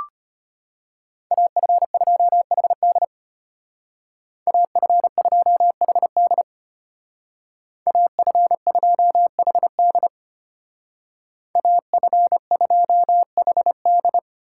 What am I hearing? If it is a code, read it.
Morse code, heard as a keyed tone, about 33 wpm, 700 Hz. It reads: AF2HD AF2HD AF2HD AF2HD